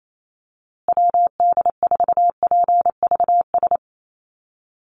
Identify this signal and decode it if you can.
Morse code, heard as a keyed tone, about 28 wpm, 700 Hz. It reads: WD4PVS
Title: USA Call Signs Repeated 2x 28wpm